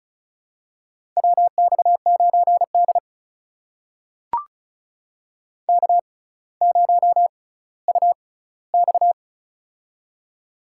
Morse code, heard as a keyed tone, about 34 wpm, spaced as for 18 wpm, 700 Hz. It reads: WX9D K0UX
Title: USA Call Signs - 4 Characters - 6x Character Spacing - 35wpm